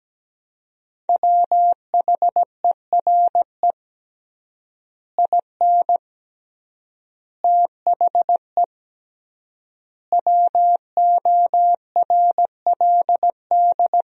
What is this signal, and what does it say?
Morse code, heard as a keyed tone, about 17 wpm, 700 Hz. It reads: WHERE IN THE WORLD
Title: 3x Word-Spacing - Sentences from Top 200 Words - 17wpm